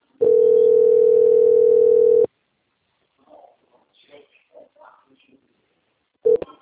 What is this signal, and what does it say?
Ringback tone (North America), 440 + 480 Hz.
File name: out- - - - - .22046